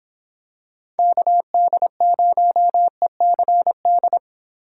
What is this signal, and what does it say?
Morse code, heard as a keyed tone, about 26 wpm, 700 Hz. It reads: KD0ECD